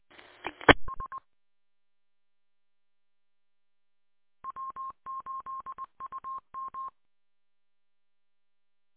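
Morse code, heard as a keyed tone, about 20 wpm, 1060 Hz. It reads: ES W8UM
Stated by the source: Morse code from W8UM on 145.230.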